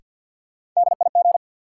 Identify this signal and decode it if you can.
Morse code, heard as a keyed tone, about 52 wpm, 700 Hz. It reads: DIG
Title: Sentences from Top 1600 Words - New Word Drill - 50wpm